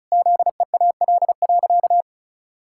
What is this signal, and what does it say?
Morse code, heard as a keyed tone, about 34 wpm, 700 Hz. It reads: ZEAL.
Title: Pangram Challenge v2 - 35wpm